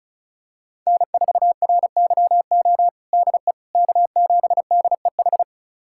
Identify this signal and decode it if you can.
Morse code, heard as a keyed tone, about 34 wpm, 700 Hz. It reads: N4RYO DE K7DEH
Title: USA Callsign DE Callsign - 5 Characters - 35wpm